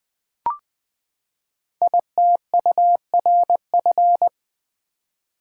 Morse code, heard as a keyed tone, about 20 wpm, 700 Hz. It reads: ITURF
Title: Code Groups - 5 Letters - 20wpm